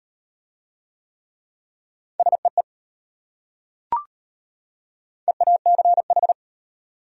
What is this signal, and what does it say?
Morse code, heard as a keyed tone, about 33 wpm, 700 Hz. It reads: SEE EACH